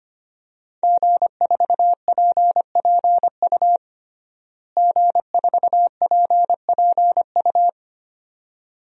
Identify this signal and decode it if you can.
Morse code, heard as a keyed tone, about 25 wpm, 700 Hz. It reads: G4PPU G4PPU
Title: DXCC-100 Call Signs - 5 Characters - Repeated 2x - 25wpm